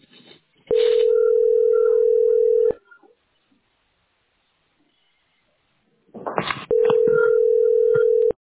Ringback tone (North America), 440 + 480 Hz.